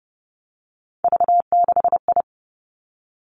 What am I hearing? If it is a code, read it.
Morse code, heard as a keyed tone, about 30 wpm, 700 Hz. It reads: V6I
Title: Three Challenging Letter-Number Combinations 30wpm